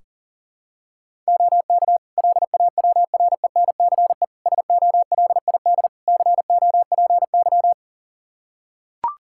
Morse code, heard as a keyed tone, about 40 wpm, 700 Hz. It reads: OK LAWRENCE SOLID COPY
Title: Practice QSO - Exchange 3 - More Info - 40wpm